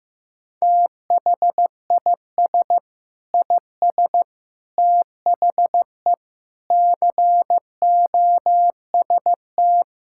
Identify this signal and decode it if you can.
Morse code, heard as a keyed tone, about 15 wpm, 700 Hz. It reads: THIS IS THE COST
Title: Sentences from Top 1000 Words - Review Part 5 - 15wpm